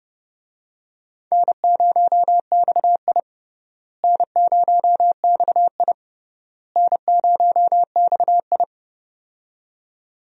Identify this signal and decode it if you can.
Morse code, heard as a keyed tone, about 30 wpm, 700 Hz. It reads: N0XI N0XI N0XI
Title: USA Call Signs - 4 Characters - Repeated 3x - 30wpm